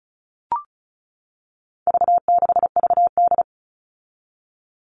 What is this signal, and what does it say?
Morse code, heard as a keyed tone, about 34 wpm, 700 Hz. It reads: V6VD
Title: All Cut Numbers - 4-Digit 35wpm